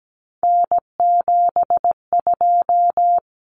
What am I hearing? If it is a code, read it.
Morse code, heard as a keyed tone, about 17 wpm, 690 Hz. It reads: N72
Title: Common Cut Numbers - 3-Digit 17wpm